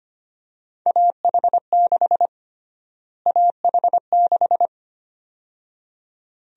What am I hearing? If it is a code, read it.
Morse code, heard as a keyed tone, about 25 wpm, 700 Hz. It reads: AH6 AH6
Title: USA Call Sign Prefixes - Repeated 2x - 25wpm